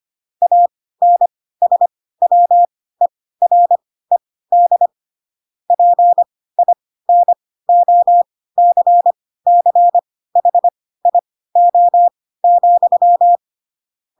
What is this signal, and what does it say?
Morse code, heard as a keyed tone, about 25 wpm, spaced as for 16 wpm, 700 Hz. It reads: ANSWERED PINOCCHIO,
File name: Pinocchio / Pinocchio ch7 10 25 ik1hge